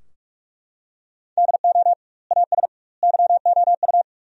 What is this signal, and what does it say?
Morse code, heard as a keyed tone, about 41 wpm, 700 Hz. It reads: DO AS YOU
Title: 2x Word-Spacing - Sentences from Top 100 Words - 45wpm